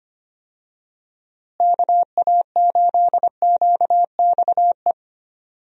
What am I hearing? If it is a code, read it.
Morse code, heard as a keyed tone, about 25 wpm, 700 Hz. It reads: KA8QXE
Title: USA Call Signs - 6 Characters - Repeated 3x - 25wpm